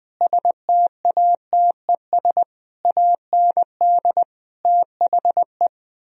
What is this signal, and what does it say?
Morse code, heard as a keyed tone, about 20 wpm, 700 Hz. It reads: STATES AND THE